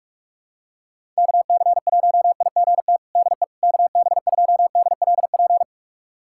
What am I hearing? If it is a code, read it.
Morse code, heard as a keyed tone, about 43 wpm, 700 Hz. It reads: KC1IGT DE KB2DLP
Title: USA Callsign DE Callsign - 6 Characters - 45wpm